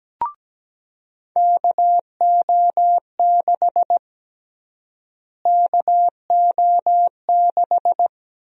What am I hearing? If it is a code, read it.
Morse code, heard as a keyed tone, about 17 wpm, 700 Hz. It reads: KO6 KO6
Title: USA Call Sign Prefixes - Repeated 2x - 17wpm